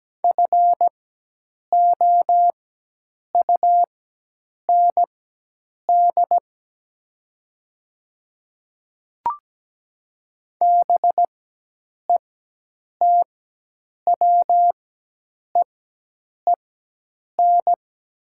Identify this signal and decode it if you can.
Morse code, heard as a keyed tone, about 17 wpm, spaced as for 8 wpm, 700 Hz. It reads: FOUND BETWEEN
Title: 4x Character Spacing - Top 300 Words - 17wpm